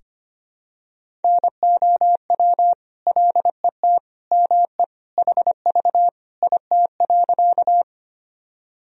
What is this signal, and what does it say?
Morse code, heard as a keyed tone, about 25 wpm, 700 Hz. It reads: NOW LET ME HV IT.
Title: Sentences from Top 300 Words and Abbreviations 25wpm